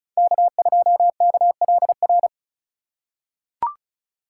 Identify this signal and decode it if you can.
Morse code, heard as a keyed tone, about 34 wpm, 700 Hz. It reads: K2KLR
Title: USA Call Signs - 5 Characters - 4x Character Spacing - 35wpm